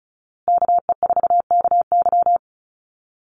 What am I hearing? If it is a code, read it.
Morse code, heard as a keyed tone, about 34 wpm, 700 Hz. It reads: KE4KY
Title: Call Signs Repeated 2x - 35wpm